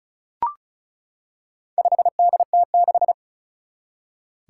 Morse code, heard as a keyed tone, about 36 wpm, 700 Hz. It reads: 5DT6
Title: Four Challenging Letter-Number Combinations 35wpm